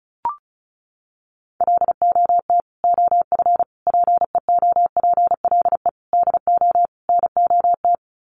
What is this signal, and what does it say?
Morse code, heard as a keyed tone, about 34 wpm, 700 Hz. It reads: LOT OF PEOPLE DO NOT
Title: Sets of 5 Words Encore 35wpm